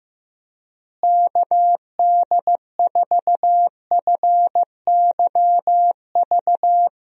Morse code, heard as a keyed tone, about 15 wpm, 700 Hz. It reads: KD4FYV
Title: USA Call Signs - 6 Characters - Repeated 3x - 15wpm